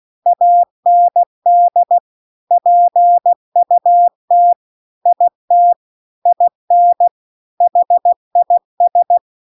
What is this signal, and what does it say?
Morse code, heard as a keyed tone, about 16 wpm, 700 Hz. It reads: AND PUT IT IN HIS